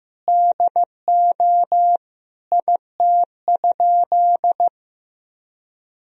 Morse code, heard as a keyed tone, about 15 wpm, 700 Hz. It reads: DO IT?
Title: Sentences from Top 100 Words and Abbreviations 15wpm